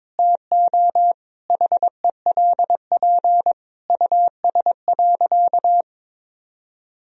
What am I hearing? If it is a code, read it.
Morse code, heard as a keyed tone, about 22 wpm, 700 Hz. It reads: TO HELP US.
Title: Sentences from Top 1000 Words - Review Part 2 - 22wpm